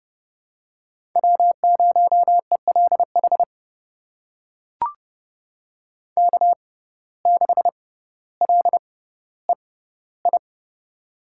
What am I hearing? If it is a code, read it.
Morse code, heard as a keyed tone, about 30 wpm, spaced as for 15 wpm, 700 Hz. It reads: W0ELH K6LEI
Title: USA Call Signs - 5 Characters - 6x Character Spacing - 30wpm